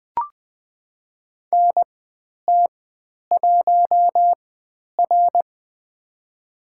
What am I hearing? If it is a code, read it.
Morse code, heard as a keyed tone, about 20 wpm, spaced as for 10 wpm, 700 Hz. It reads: NT1R
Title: 095 Lesson 16g - Letter c - Call Signs 20wpm